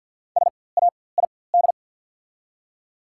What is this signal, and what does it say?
Morse code, heard as a keyed tone, about 48 wpm, spaced as for 23 wpm, 700 Hz. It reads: SAID
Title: 4x Character Spacing - Top 300 Words - 50wpm